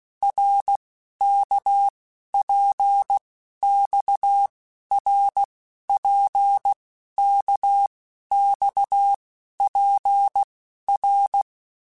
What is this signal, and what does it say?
Morse code, heard as a keyed tone, about 16 wpm, 800 Hz. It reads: R K P X R P K X P R
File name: Basic Letter Groups By7Groups / RKPX / RKPX@15